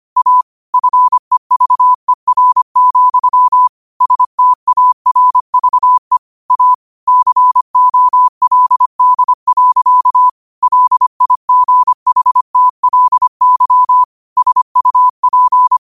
Morse code, heard as a keyed tone, about 25 wpm, 1000 Hz. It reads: A FEVER, STARVE A COLD. LIGHTLY SUP
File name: qotdfiles / 25wpm